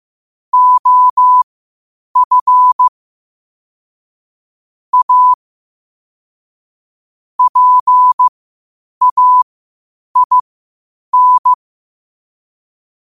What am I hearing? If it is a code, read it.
Morse code, heard as a keyed tone, about 15 wpm, spaced as for 8 wpm, 1000 Hz. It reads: OF A PAIN
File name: 8wpm